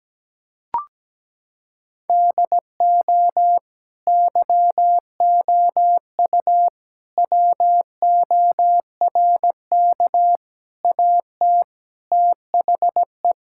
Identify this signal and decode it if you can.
Morse code, heard as a keyed tone, about 17 wpm, 700 Hz. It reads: DO YOU WORK AT THE
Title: Sentences from Top 1000 Words - Review Part 5 - 17wpm